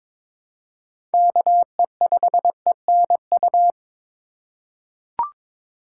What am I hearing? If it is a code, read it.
Morse code, heard as a keyed tone, about 21 wpm, 700 Hz. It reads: KE5ENU
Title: USA Call Signs - 6 Characters - 3x Character Spacing - 22wpm